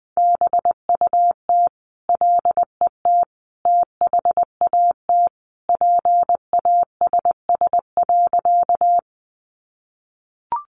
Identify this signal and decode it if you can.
Morse code, heard as a keyed tone, about 20 wpm, 700 Hz. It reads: BUT LET THAT PASS.
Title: Sentences from Top 400 Words 20wpm